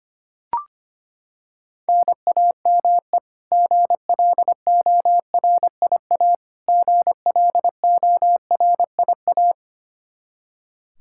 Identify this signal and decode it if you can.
Morse code, heard as a keyed tone, about 25 wpm, 700 Hz. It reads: NAME GLORIA GLORIA
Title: K1USN SST QSO Practice 25wpm